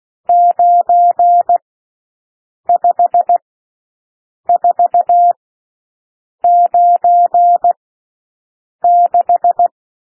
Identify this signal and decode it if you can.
Morse code, heard as a keyed tone, about 16 wpm, spaced as for 6 wpm, 700 Hz. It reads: 95496
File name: SOUND NUMBERS 16 wpm